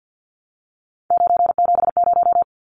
Morse code, heard as a keyed tone, about 50 wpm, 700 Hz. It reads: 970
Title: Numbers - 3-Digit 50wpm